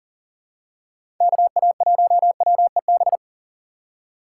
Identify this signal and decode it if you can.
Morse code, heard as a keyed tone, about 40 wpm, 700 Hz. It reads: KA1WEB
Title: USA Call Signs - 3x Character Spacing - 40wpm